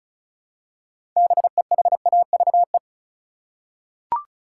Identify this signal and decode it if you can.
Morse code, heard as a keyed tone, about 36 wpm, 700 Hz. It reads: BEHAVE
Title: Six Letter Words - 35wpm